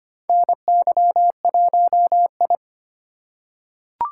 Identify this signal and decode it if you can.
Morse code, heard as a keyed tone, about 25 wpm, 700 Hz. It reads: NY1I
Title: USA Call Signs - 4 Characters - 3x Character Spacing - 25wpm